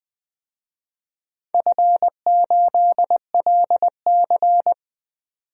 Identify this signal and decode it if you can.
Morse code, heard as a keyed tone, about 20 wpm, 700 Hz. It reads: F8LC
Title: DXCC-100 Call Signs - 4 Characters - 20wpm